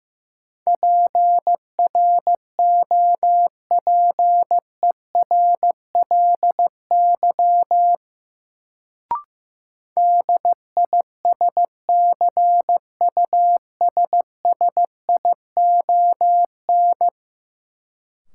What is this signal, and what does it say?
Morse code, heard as a keyed tone, about 15 wpm, 700 Hz. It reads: PROPERLY DISCUSSION
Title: Sentences from Top 1100 Words - New Word Drill Alt - 15wpm